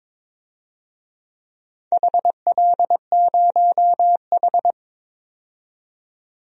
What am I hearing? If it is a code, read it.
Morse code, heard as a keyed tone, about 23 wpm, 700 Hz. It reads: HL0H
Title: DXCC-100 Call Signs - 4 Characters - 22wpm